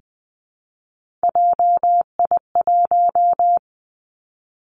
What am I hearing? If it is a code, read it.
Morse code, heard as a keyed tone, about 20 wpm, 700 Hz. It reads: JI1
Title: Three Challenging Letter-Number Combinations 20wpm